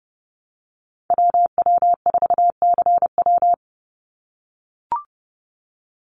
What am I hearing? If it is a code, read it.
Morse code, heard as a keyed tone, about 30 wpm, 700 Hz. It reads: WW4CW
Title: USA Call Signs 30wpm